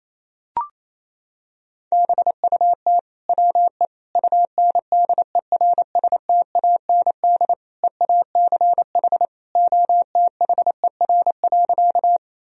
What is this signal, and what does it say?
Morse code, heard as a keyed tone, about 28 wpm, 700 Hz. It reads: BUT WE UNDERSTAND EACH OTHER.
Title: Sentences from Top 1000 Words - Review Part 1 - 28wpm